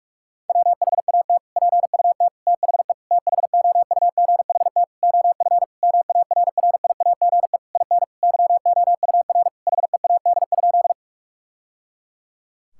Morse code, heard as a keyed tone, about 42 wpm, 700 Hz. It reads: WHAT PUT THE THOUGHT OF MARRIAGE IN YOUR HEAD?
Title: Sentences from Top 1000 Words - Review Part 5 - 45wpm